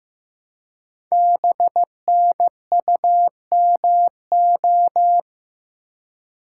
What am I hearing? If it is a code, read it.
Morse code, heard as a keyed tone, about 15 wpm, 700 Hz. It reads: BNUMO
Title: Code Groups - 5 Letters - 15wpm